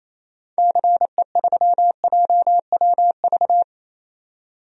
Morse code, heard as a keyed tone, about 28 wpm, 700 Hz. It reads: CE3JWV
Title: DXCC-100 Call Signs - 6 Characters - 28wpm